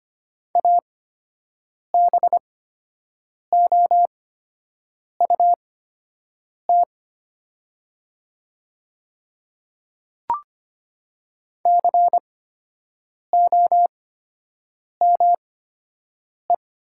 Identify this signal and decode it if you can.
Morse code, heard as a keyed tone, about 25 wpm, spaced as for 7 wpm, 690 Hz. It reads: ABOUT COME
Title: 8x Character Spacing - Top 500 Words - 25wpm